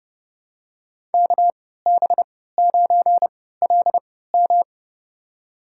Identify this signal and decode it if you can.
Morse code, heard as a keyed tone, about 32 wpm, spaced as for 17 wpm, 700 Hz. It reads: KB9LM
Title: USA Call Signs - 5 Characters - 3x Character Spacing - 30wpm